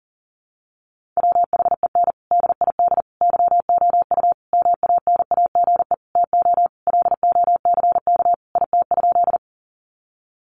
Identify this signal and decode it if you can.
Morse code, heard as a keyed tone, about 40 wpm, 700 Hz. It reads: WHEN DID YOU MANAGE TO LOCK IT?
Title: Sentences from Top 1400 Words 40wpm